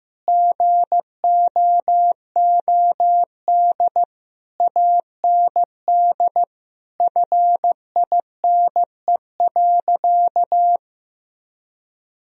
Morse code, heard as a keyed tone, about 15 wpm, 700 Hz. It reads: GOOD AND FINE.
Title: Sentences from Top 1000 Words - Review Part 2 - 15wpm